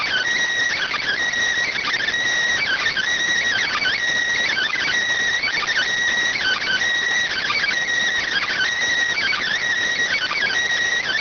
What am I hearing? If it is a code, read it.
Emergency Alert System (SAME) tones (header bursts), heard as harsh buzzing bursts. It is heard from 0.7 to 10.5 s.